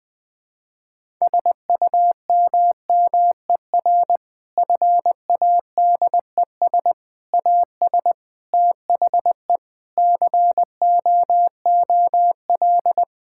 Morse code, heard as a keyed tone, about 20 wpm, 700 Hz. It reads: SUMMER FADES AS THE COOL
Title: Difficult Sentences with Hints - 20wpm